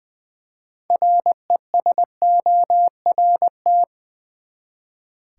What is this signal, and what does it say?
Morse code, heard as a keyed tone, about 20 wpm, 700 Hz. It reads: RESORT
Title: Six Letter Words - 20wpm